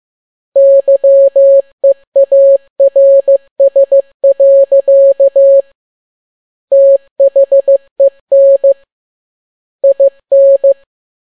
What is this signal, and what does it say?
Morse code, heard as a keyed tone, about 15 wpm, 550 Hz. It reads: YEARS. THEN IN
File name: n6tt p1 15C13